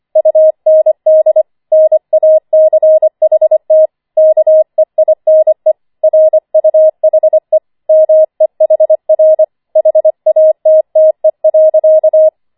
Morse code, heard as a keyed tone, about 25 wpm, 630 Hz. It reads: UND NACHT KEINE RUHE MEHR HATTE.
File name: 25wpm